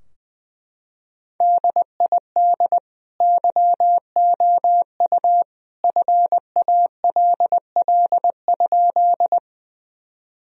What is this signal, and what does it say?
Morse code, heard as a keyed tone, about 20 wpm, 700 Hz. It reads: DID YOU FALL?
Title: Sentences from Top 1000 Words - Review Part 2 - 20wpm